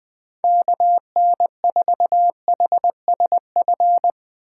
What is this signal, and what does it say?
Morse code, heard as a keyed tone, about 20 wpm, 700 Hz. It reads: KN4HSF